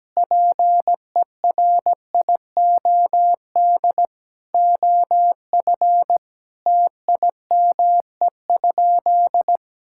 Morse code, heard as a keyed tone, about 17 wpm, 700 Hz. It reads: PERIOD OF TIME?
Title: Sentences from Top 1000 Words - Review Part 5 - 17wpm